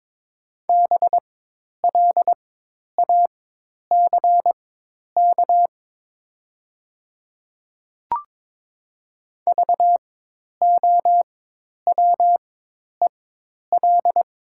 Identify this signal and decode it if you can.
Morse code, heard as a keyed tone, about 22 wpm, spaced as for 10 wpm, 700 Hz. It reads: BLACK VOWEL
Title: 4x Character Spacing - Top 500 Words - 22wpm